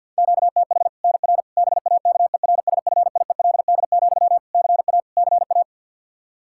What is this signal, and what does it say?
Morse code, heard as a keyed tone, about 53 wpm, 700 Hz. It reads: QTH NR BAKERSFIELD, CA CA
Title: Practice QSO - Exchange 1 - Beginning - 50wpm